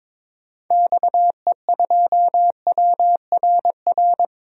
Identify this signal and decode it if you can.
Morse code, heard as a keyed tone, about 22 wpm, 700 Hz. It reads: XE2WRR
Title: DXCC-100 Call Signs - 6 Characters - Repeated 2x - 22wpm